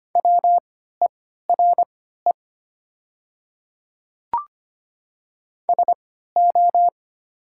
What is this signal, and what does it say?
Morse code, heard as a keyed tone, about 25 wpm, spaced as for 14 wpm, 700 Hz. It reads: WERE SO
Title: 3x Character Spacing - Top 100 Words - 25wpm